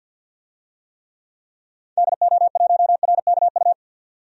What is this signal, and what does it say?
Morse code, heard as a keyed tone, about 50 wpm, 700 Hz. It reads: DO1RKU
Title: DXCC-100 Call Signs - 6 Characters - 50wpm